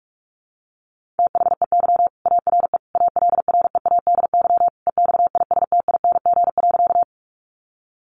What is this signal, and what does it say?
Morse code, heard as a keyed tone, about 44 wpm, 700 Hz. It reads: THEY ARE ALREADY EXISTING.
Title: Sentences from Top 1500 Words 45wpm